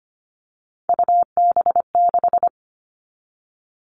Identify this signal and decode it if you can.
Morse code, heard as a keyed tone, about 25 wpm, 700 Hz. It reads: UB6
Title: All Cut Numbers - 3-Digit 25wpm